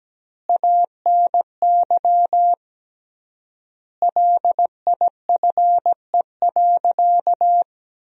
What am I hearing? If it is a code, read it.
Morse code, heard as a keyed tone, about 17 wpm, 700 Hz. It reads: ANY LIFE.
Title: 3x Word-Spacing - Sentences from Top 300 Words - 17wpm